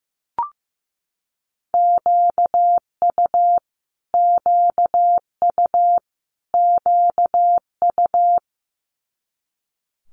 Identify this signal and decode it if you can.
Morse code, heard as a keyed tone, about 15 wpm, 700 Hz. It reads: QU QU QU
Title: Multi-letter Phonemes Repeated 3x 15wpm